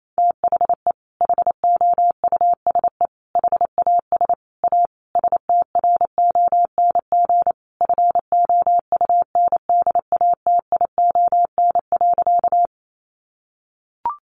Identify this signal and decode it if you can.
Morse code, heard as a keyed tone, about 28 wpm, 700 Hz. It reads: THE HOUSE HAS A STRONG FOUNDATION.
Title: Sentences from Top 1200 Words 28wpm